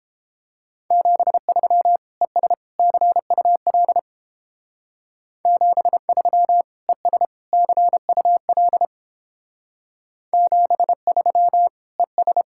Morse code, heard as a keyed tone, about 31 wpm, 700 Hz. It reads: 73 ES CUL 73 ES CUL 73 ES